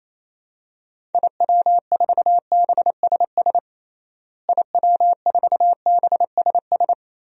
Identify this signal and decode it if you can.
Morse code, heard as a keyed tone, about 28 wpm, 700 Hz. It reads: IW4BSS IW4BSS
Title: DXCC-100 Call Signs - 6 Characters - Repeated 3x - 28wpm